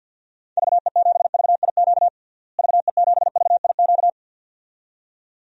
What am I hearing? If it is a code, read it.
Morse code, heard as a keyed tone, about 52 wpm, 700 Hz. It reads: VE7VIQ VE7VIQ
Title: DXCC-100 Call Signs - 6 Characters - Repeated 2x - 50wpm